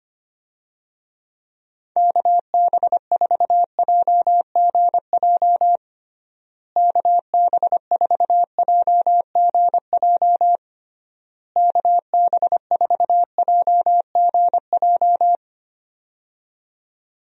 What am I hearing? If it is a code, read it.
Morse code, heard as a keyed tone, about 25 wpm, 700 Hz. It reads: KB4JGJ KB4JGJ KB4JGJ
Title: USA Call Signs - 6 Characters - Repeated 3x - 25wpm